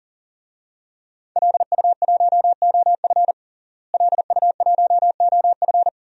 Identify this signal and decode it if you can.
Morse code, heard as a keyed tone, about 40 wpm, 700 Hz. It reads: LU1OF LU1OF
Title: DXCC-100 Call Signs - 5 Characters - Repeated 3x - 40wpm